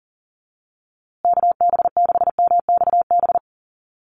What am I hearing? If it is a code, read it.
Morse code, heard as a keyed tone, about 40 wpm, 700 Hz. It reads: KB6MXB
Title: USA Call Signs Repeated 2x 40wpm